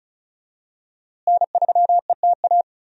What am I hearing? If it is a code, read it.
Morse code, heard as a keyed tone, about 34 wpm, 700 Hz. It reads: N3ETA
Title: USA Call Signs - 5 Characters - Repeated 2x - 35wpm